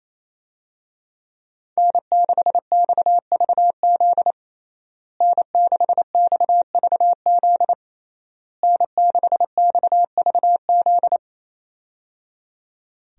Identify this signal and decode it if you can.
Morse code, heard as a keyed tone, about 28 wpm, 700 Hz. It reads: N6XVZ N6XVZ N6XVZ
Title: USA Call Signs - 5 Characters - Repeated 3x - 28wpm